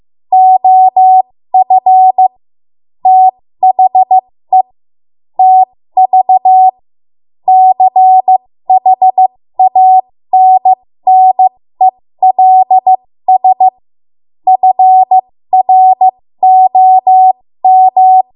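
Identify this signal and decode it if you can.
Morse code, heard as a keyed tone, about 15 wpm, spaced as for 13 wpm, 750 Hz. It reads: OF THE TV CHANNELS FROM
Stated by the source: 13 wpm Morse code practice archive transmission.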